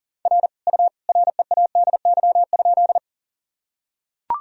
Morse code, heard as a keyed tone, about 40 wpm, 700 Hz. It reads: R U READY?
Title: Sentences from Top 400 Words and Abbreviations 40wpm